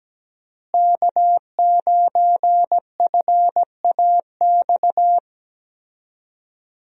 Morse code, heard as a keyed tone, about 17 wpm, 700 Hz. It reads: K9FAX
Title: USA Call Signs - 5 Characters - Repeated 3x - 17wpm